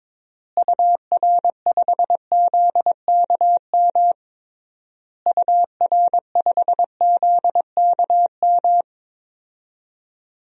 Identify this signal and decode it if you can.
Morse code, heard as a keyed tone, about 22 wpm, 700 Hz. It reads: UR5ZKM UR5ZKM
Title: DXCC-100 Call Signs - 6 Characters - Repeated 3x - 22wpm